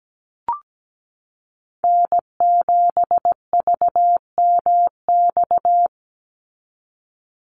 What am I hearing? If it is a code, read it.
Morse code, heard as a keyed tone, about 17 wpm, 700 Hz. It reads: N7VMX
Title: USA Call Signs 17wpm